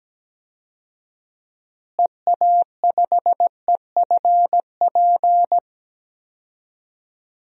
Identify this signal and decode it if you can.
Morse code, heard as a keyed tone, about 17 wpm, 700 Hz. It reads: EA5EFP
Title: DXCC-100 Call Signs - 6 Characters - Repeated 2x - 17wpm